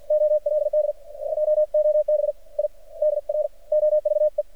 Morse code, heard as a keyed tone, about 49 wpm, spaced as for 42 wpm, 600 Hz. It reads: OLD JOB I DROVE
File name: archived / CRSnet